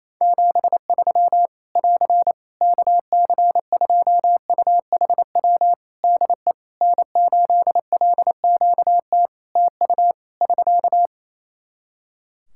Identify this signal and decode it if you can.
Morse code, heard as a keyed tone, about 28 wpm, 700 Hz. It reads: 73 <AR> KC2UHW DE N8LQT TU <SK>
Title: Practice QSO - Exchange 4 - Ending - 28wpm